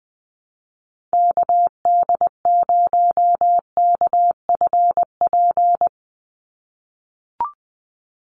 Morse code, heard as a keyed tone, about 20 wpm, 700 Hz. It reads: KD0KFP
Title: USA Call Signs 20wpm